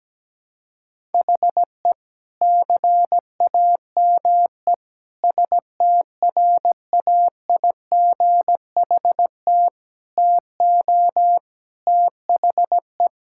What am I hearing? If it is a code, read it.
Morse code, heard as a keyed tone, about 17 wpm, 700 Hz. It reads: HE CAME STRAIGHT TO THE
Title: Sentences from Top 1000 Words - Review Part 6 - 17wpm